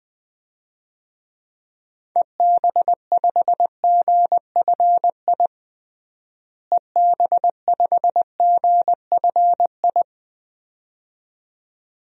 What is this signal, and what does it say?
Morse code, heard as a keyed tone, about 20 wpm, 700 Hz. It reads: EB5GFI EB5GFI
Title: DXCC-100 Call Signs - 6 Characters - Repeated 2x - 20wpm